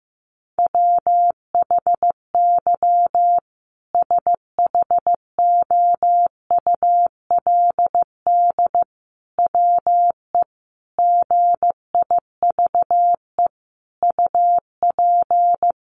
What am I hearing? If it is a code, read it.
Morse code, heard as a keyed tone, about 15 wpm, 700 Hz. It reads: WHY SHOULD WE GIVE UP
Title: Sentences from Top 700 Words 15wpm